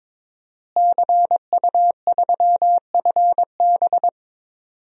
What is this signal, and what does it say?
Morse code, heard as a keyed tone, about 22 wpm, 700 Hz. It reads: CU3FB
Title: DXCC-100 Call Signs - 5 Characters - Repeated 2x - 22wpm